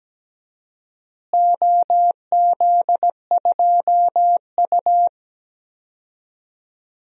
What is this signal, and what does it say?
Morse code, heard as a keyed tone, about 17 wpm, 700 Hz. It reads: OZ2U
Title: DXCC-100 Call Signs - 4 Characters - Repeated 2x - 17wpm